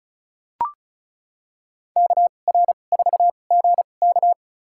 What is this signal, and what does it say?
Morse code, heard as a keyed tone, about 34 wpm, spaced as for 25 wpm, 700 Hz. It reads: KR4GK
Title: USA Call Signs - 5 Characters - 2x Character Spacing - 35wpm